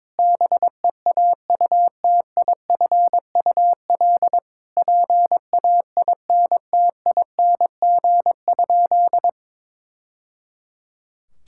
Morse code, heard as a keyed tone, about 22 wpm, 700 Hz. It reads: BEAUTIFUL PAINTING?
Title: Sentences from Top 1000 Words - Review Part 6 - 22wpm